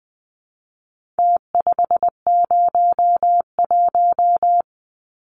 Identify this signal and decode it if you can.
Morse code, heard as a keyed tone, about 20 wpm, 700 Hz. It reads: T501
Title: Numbers - Contest Style - 4-Digit 20wpm